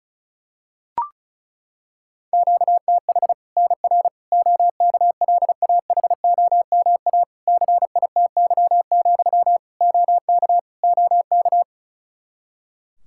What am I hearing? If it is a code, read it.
Morse code, heard as a keyed tone, about 36 wpm, 700 Hz. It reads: QTH NR OKLAHOMA CITY, OK OK
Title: Practice QSO - Exchange 1 - Beginning - 35wpm